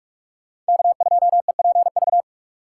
Morse code, heard as a keyed tone, about 46 wpm, 700 Hz. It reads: K2EPV
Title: USA Call Signs - 5 Characters - 4x Character Spacing - 45wpm